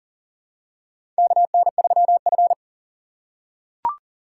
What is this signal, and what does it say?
Morse code, heard as a keyed tone, about 40 wpm, 700 Hz. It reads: KN3F
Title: USA Call Signs - 4 Characters - 4x Character Spacing - 40wpm